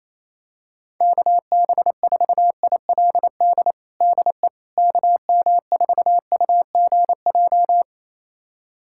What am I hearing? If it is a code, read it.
Morse code, heard as a keyed tone, about 28 wpm, 700 Hz. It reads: KB4ILD DE KM4UGJ
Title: USA Callsign DE Callsign - 6 Characters - 28wpm